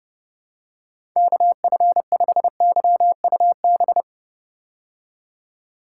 Morse code, heard as a keyed tone, about 30 wpm, 700 Hz. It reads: KF5YUB
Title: K1USN SST QSO Practice - 30wpm